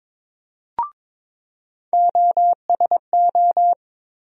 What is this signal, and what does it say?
Morse code, heard as a keyed tone, about 22 wpm, 700 Hz. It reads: OSO
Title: USA Call Sign Postfixes - 3 Letters 22wpm